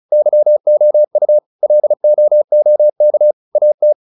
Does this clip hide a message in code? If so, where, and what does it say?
Yes: Morse code from 0.1 to 3.9 s, heard as a keyed tone, about 34 wpm, 600 Hz. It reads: YOU LOOK AT